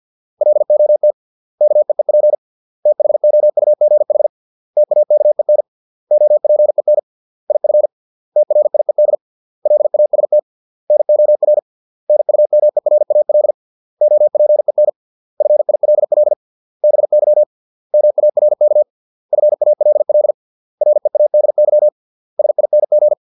Morse code, heard as a keyed tone, about 50 wpm, 600 Hz. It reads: LOT KEEP THOUGH TAKEN OPEN IF TRIED LAST NOR NUMERAL OPEN FILL BY MARK FALL READY SING